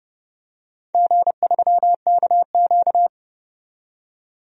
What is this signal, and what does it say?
Morse code, heard as a keyed tone, about 31 wpm, 700 Hz. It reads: G3KQ
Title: DXCC-100 Call Signs - 4 Characters - 30wpm